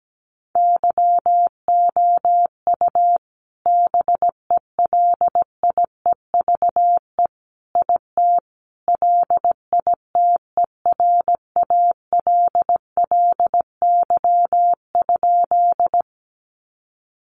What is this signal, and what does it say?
Morse code, heard as a keyed tone, about 17 wpm, 700 Hz. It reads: YOU BELIEVE IT LITERALLY?
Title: Sentences from Top 1400 Words 17wpm